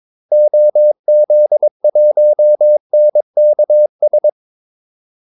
Morse code, heard as a keyed tone, about 22 wpm, 600 Hz. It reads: OZ1NKS
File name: Callsigns-22wpm